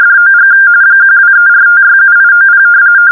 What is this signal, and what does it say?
RTTY, about 45.45 baud. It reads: DOG 1234567890
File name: RTTY 170Hz 45point45-01